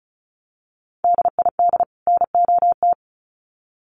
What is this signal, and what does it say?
Morse code, heard as a keyed tone, about 34 wpm, 700 Hz. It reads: DID NOT
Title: Sets of 2 Words 35wpm